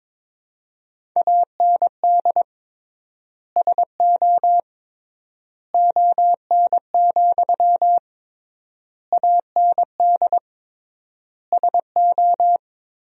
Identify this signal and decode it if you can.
Morse code, heard as a keyed tone, about 21 wpm, 700 Hz. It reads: AND SO ON, AND SO
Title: 3x Word-Spacing - Sentences from Top 100 Words - 22wpm